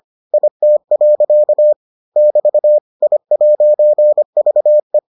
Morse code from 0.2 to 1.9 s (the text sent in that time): IT.